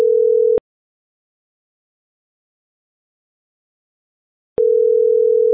Ringback tone (North America), 440 + 480 Hz.